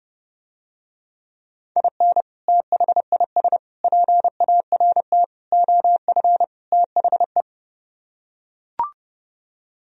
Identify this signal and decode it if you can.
Morse code, heard as a keyed tone, about 30 wpm, 700 Hz. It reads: IN THIS PART OF THE